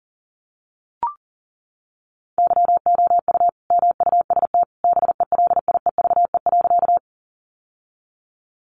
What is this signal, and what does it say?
Morse code, heard as a keyed tone, about 40 wpm, 700 Hz. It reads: YOU MUST BELIEVE.
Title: Sentences from Top 1000 Words 40wpm